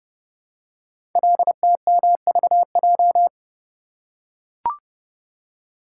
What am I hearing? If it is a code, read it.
Morse code, heard as a keyed tone, about 29 wpm, 700 Hz. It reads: LTMVJ
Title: Code Groups - 5 Letters - 30wpm